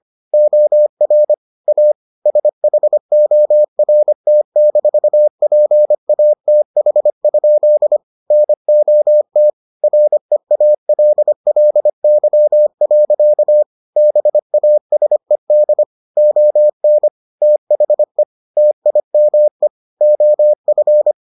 Morse code, heard as a keyed tone, about 25 wpm, 600 Hz. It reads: OR A SHORT-PATH? NOT REALLY. BASED ON THE TIME OF